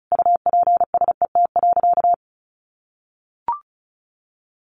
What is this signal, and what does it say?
Morse code, heard as a keyed tone, about 34 wpm, 700 Hz. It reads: UPSET.
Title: Sentences from Top 1200 Words 35wpm